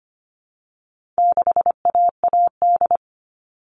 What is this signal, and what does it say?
Morse code, heard as a keyed tone, about 25 wpm, 700 Hz. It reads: 6AAD
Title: All Cut Numbers - 4-Digit 25wpm